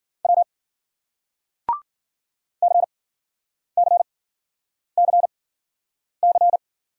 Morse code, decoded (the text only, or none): W CCCC